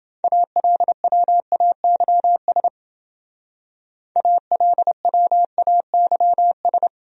Morse code, heard as a keyed tone, about 30 wpm, 700 Hz. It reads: ALWAYS ALWAYS